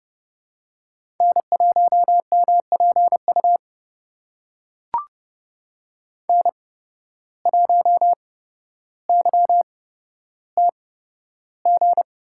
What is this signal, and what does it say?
Morse code, heard as a keyed tone, about 30 wpm, spaced as for 13 wpm, 700 Hz. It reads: N1MPU N1YTG
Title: USA Call Signs - 5 Characters - 8x Character Spacing - 30wpm